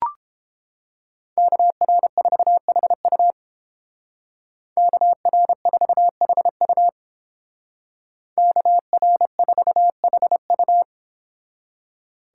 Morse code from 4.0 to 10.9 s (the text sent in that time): KR4HU KR4HU